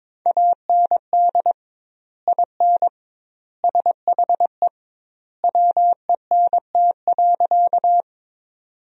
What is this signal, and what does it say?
Morse code, heard as a keyed tone, about 22 wpm, 700 Hz. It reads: AND IN SHE WENT.
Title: 2x Word-Spacing - Sentences from Top 200 Words - 22wpm